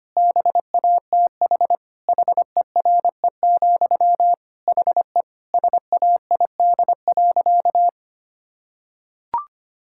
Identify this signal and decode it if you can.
Morse code, heard as a keyed tone, about 25 wpm, 700 Hz. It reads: BATH HERE, HE SAID.
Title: Sentences from Top 1600 Words 25wpm